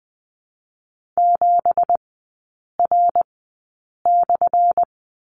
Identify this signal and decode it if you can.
Morse code, heard as a keyed tone, about 20 wpm, spaced as for 8 wpm, 700 Hz. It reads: 7R/
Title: 205 Lesson 33e - Letter j - Three Characters 20wpm